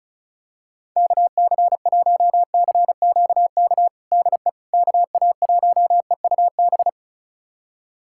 Morse code, heard as a keyed tone, about 36 wpm, 700 Hz. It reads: KC1CQK DE KA1EUB
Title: USA Callsign DE Callsign 35wpm